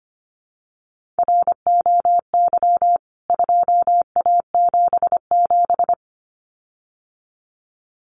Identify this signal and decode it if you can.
Morse code, heard as a keyed tone, about 25 wpm, 690 Hz. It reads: ROY 2A77
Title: CWT Exchange 25wpm